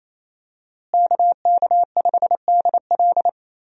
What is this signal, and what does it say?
Morse code, heard as a keyed tone, about 27 wpm, 700 Hz. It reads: KK5DL
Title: USA Call Signs - 5 Characters - Repeated 3x - 28wpm